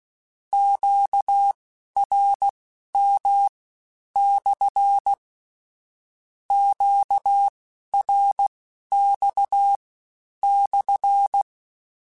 Morse code, heard as a keyed tone, about 16 wpm, spaced as for 10 wpm, 800 Hz. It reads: QRM/ QRX/
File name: Q Calls / Q Calls-mp3s / Q Calls SPSL1 @15